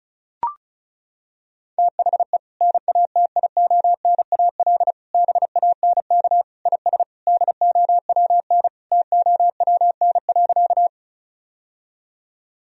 Morse code, heard as a keyed tone, about 36 wpm, 700 Hz. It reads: THE NATIONAL BANK IS DOWN TOWN.
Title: Sentences from Top 1000 Words - Review Part 4 - 35wpm